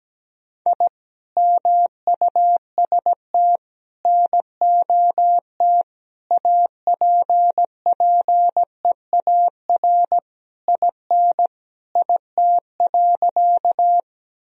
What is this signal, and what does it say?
Morse code, heard as a keyed tone, about 17 wpm, 700 Hz. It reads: I MUST NOT APPEAR IN IT.
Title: Sentences from Top 1000 Words - Review Part 2 - 17wpm